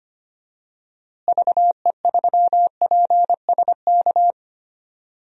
Morse code, heard as a keyed tone, about 25 wpm, 690 Hz. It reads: VE3PSK
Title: DXCC-100 Call Signs - 6 Characters - Repeated 3x - 25wpm